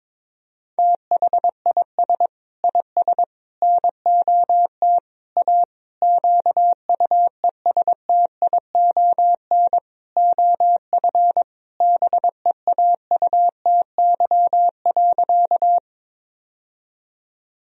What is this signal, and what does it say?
Morse code, heard as a keyed tone, about 22 wpm, 700 Hz. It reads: THIS IS NOT A QUESTION OF BEAUTY.
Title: Sentences from Top 1000 Words - Review Part 2 - 22wpm